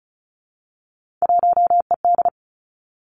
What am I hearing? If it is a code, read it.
Morse code, heard as a keyed tone, about 36 wpm, 700 Hz. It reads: 1ED
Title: Three Challenging Letter-Number Combinations 35wpm